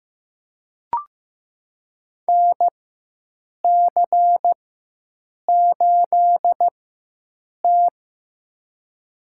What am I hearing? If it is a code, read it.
Morse code, heard as a keyed tone, about 15 wpm, spaced as for 7 wpm, 700 Hz. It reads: NC8T
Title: USA Call Signs - 4x Character Spacing - 15wpm